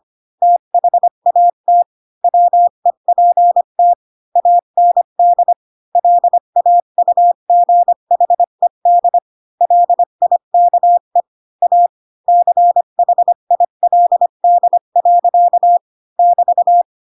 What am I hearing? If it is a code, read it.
Morse code, heard as a keyed tone, about 25 wpm, 700 Hz. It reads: THAT WEPT AND LAUGHED LIKE A CHILD. <BT>
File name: Pinocchio / Pinocchio ch1 20 25 ik1hge